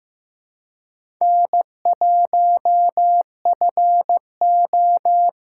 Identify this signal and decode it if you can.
Morse code, heard as a keyed tone, about 15 wpm, 700 Hz. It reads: N1FO
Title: USA Callsign DE Callsign - 4 Characters - 15wpm